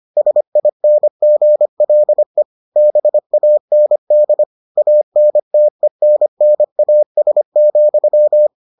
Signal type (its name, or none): Morse code